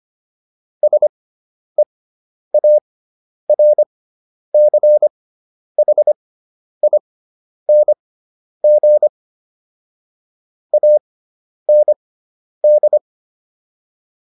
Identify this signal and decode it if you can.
Morse code, heard as a keyed tone, about 25 wpm, spaced as for 10 wpm, 600 Hz. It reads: SEARCHING AND